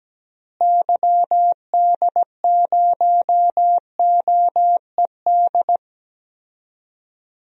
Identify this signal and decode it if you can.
Morse code, heard as a keyed tone, about 17 wpm, 700 Hz. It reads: YD0OED
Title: DXCC-100 Call Signs - 6 Characters - Repeated 3x - 17wpm